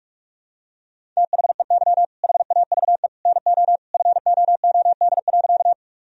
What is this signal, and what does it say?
Morse code, heard as a keyed tone, about 46 wpm, 700 Hz. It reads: THEY HAVE NO FOOD.
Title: Sentences from Top 1000 Words - Review Part 1 - 45wpm